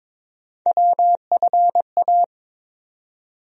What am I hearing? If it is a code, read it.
Morse code, heard as a keyed tone, about 22 wpm, 700 Hz. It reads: WFA
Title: USA Call Sign Postfixes - 3 Letters 22wpm